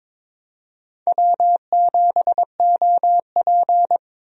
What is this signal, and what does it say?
Morse code, heard as a keyed tone, about 22 wpm, 700 Hz. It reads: W7OP
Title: USA Call Signs - 4 Characters - Repeated 3x - 22wpm